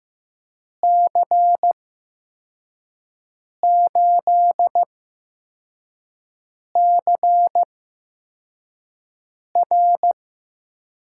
Morse code, heard as a keyed tone, about 15 wpm, spaced as for 4 wpm, 700 Hz. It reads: C8CR
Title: USA Call Signs - 8x Character Spacing - 15wpm